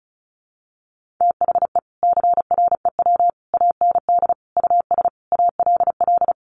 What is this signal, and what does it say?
Morse code, heard as a keyed tone, about 34 wpm, 700 Hz. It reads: THE CREW AND US ALL
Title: Sentences from Top 1500 Words 35wpm